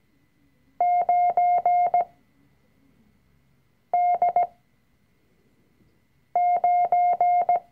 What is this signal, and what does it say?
Morse code, heard as a keyed tone, about 17 wpm, spaced as for 4 wpm, 710 Hz. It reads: 9D9
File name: CW-Online-Kurs / Anfänger / Lektion 005